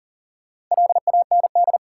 Morse code, heard as a keyed tone, about 38 wpm, 700 Hz. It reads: LAND
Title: Sentences from Top 500 Words - New Word Drill - 40wpm